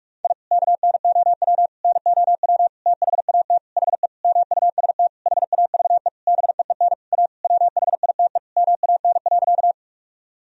Morse code, heard as a keyed tone, about 41 wpm, 700 Hz. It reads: I KNOW NOW THAT HE MUST HAVE BEEN A WHITE MAN.
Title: Sentences from Top 1000 Words - Review Part 1 - 45wpm